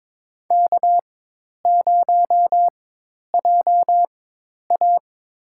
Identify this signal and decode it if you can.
Morse code, heard as a keyed tone, about 22 wpm, spaced as for 10 wpm, 700 Hz. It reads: K0JA
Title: USA Call Signs - 4x Character Spacing - 22wpm